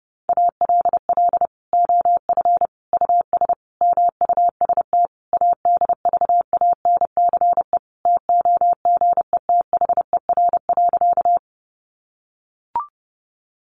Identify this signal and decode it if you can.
Morse code, heard as a keyed tone, about 28 wpm, 700 Hz. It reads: ALL OF US MUST ADVANCE TOGETHER.
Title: Sentences from Top 1400 Words 30wpm